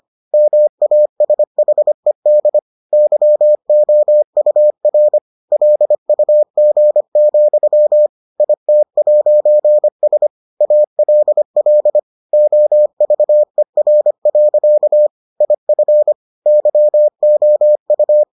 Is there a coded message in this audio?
Yes: Morse code from 0.3 to 18.2 s, heard as a keyed tone, about 25 wpm, 600 Hz. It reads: MASHED YOUR LUG, IT'S ALL OVER. IF YOU